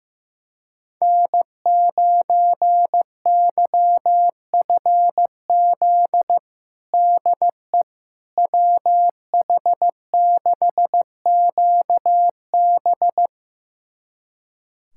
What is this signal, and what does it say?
Morse code, heard as a keyed tone, about 15 wpm, 700 Hz. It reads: N9YFZ DE WH6QB
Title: USA Callsign DE Callsign - 5 Characters - 15wpm